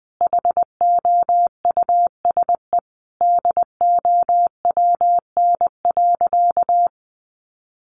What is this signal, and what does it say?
Morse code, heard as a keyed tone, about 20 wpm, 700 Hz. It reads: HOUSE DOWN.
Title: Sentences from Top 1300 Words 20wpm